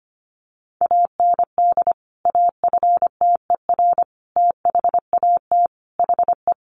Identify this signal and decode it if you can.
Morse code, heard as a keyed tone, about 25 wpm, 700 Hz. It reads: AND AFTER THAT HE
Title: Sentences from Top 1400 Words 25wpm